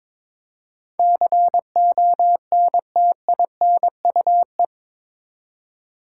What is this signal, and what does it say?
Morse code, heard as a keyed tone, about 21 wpm, 700 Hz. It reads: CONTINUE
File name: practice / 22wpm / Eight Letter Words - 22wpm